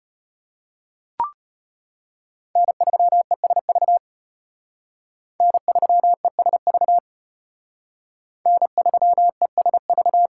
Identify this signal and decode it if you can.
Morse code, heard as a keyed tone, about 31 wpm, 700 Hz. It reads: N3ESV N3ESV N3ESV
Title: Speed Racing - USA 1x3 Call Signs 25wpm